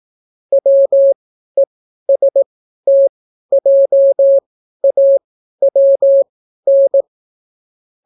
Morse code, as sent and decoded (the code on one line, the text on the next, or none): .-- . ... - .--- .- .-- -.
WESTJAWN